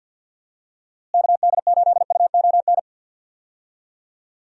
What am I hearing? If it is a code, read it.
Morse code, heard as a keyed tone, about 49 wpm, 700 Hz. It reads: KD8UON
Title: USA Call Signs - 6 Characters - Repeated 2x - 50wpm